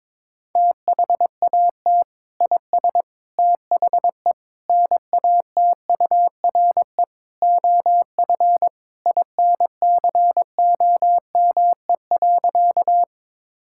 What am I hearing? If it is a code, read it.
Morse code, heard as a keyed tone, about 22 wpm, 700 Hz. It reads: THAT IS THE NATURE OF INCOME.
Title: Sentences from Top 1000 Words - Review Part 5 - 22wpm